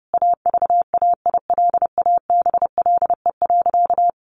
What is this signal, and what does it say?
Morse code, heard as a keyed tone, about 30 wpm, 700 Hz. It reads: AVAILABLE.
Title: Sentences from Top 1100 Words 30wpm